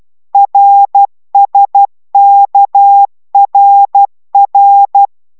Morse code, heard as a keyed tone, about 12 wpm, 800 Hz. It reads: RSKRR